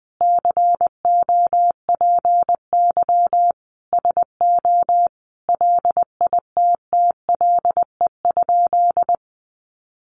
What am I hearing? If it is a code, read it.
Morse code, heard as a keyed tone, about 20 wpm, 700 Hz. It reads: COPY SO LITTLE?
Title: Sentences from Top 1400 Words 20wpm